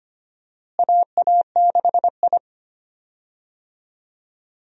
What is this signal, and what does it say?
Morse code, heard as a keyed tone, about 25 wpm, 700 Hz. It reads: AA6I
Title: USA Call Signs - 4 Characters - Repeated 3x - 25wpm